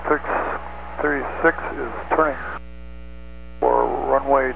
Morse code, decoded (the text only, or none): none